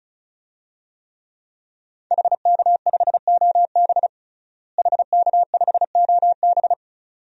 Morse code, heard as a keyed tone, about 34 wpm, 700 Hz. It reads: HK5OB HK5OB
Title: DXCC-100 Call Signs - 5 Characters - Repeated 2x - 35wpm